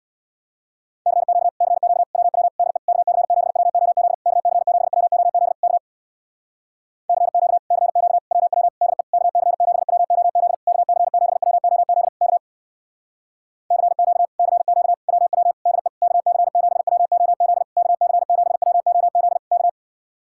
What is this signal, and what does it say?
Morse code, heard as a keyed tone, about 66 wpm, 700 Hz. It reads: CQ CQ WW DE KC7WOZ KC7WOZ K CQ CQ WW DE KC7WOZ KC7WOZ K CQ CQ WW DE KC7WOZ KC7WOZ K